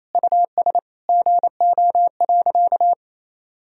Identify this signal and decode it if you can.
Morse code, heard as a keyed tone, about 27 wpm, 700 Hz. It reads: US GO.
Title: Sentences from Top 1000 Words - Review Part 2 - 28wpm